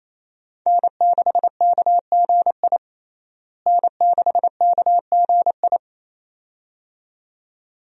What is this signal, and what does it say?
Morse code, heard as a keyed tone, about 28 wpm, 700 Hz. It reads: N6KGI N6KGI
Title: USA Call Signs - 5 Characters - Repeated 2x - 28wpm